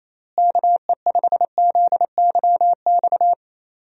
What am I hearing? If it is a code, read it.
Morse code, heard as a keyed tone, about 28 wpm, 700 Hz. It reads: KE5ZYX
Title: USA Call Signs - 6 Characters - Repeated 2x - 28wpm